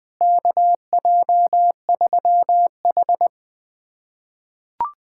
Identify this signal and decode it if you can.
Morse code, heard as a keyed tone, about 20 wpm, 700 Hz. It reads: KJ3H
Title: USA Callsign DE Callsign - 4 Characters - 20wpm